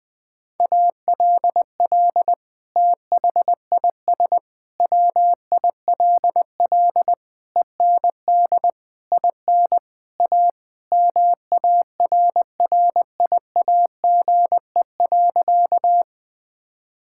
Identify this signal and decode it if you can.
Morse code, heard as a keyed tone, about 21 wpm, 700 Hz. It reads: ALL THIS WILL END IN A MARRIAGE.
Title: Sentences from Top 1000 Words - Review Part 5 - 20wpm